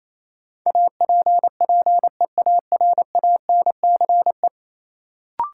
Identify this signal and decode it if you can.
Morse code, heard as a keyed tone, about 28 wpm, 700 Hz. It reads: APPEARANCE
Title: Sentences from Top 1100 Words - New Word Drill Alt - 28wpm